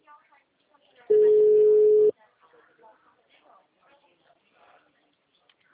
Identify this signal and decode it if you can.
Ringback tone (Europe, 425 Hz), with speech.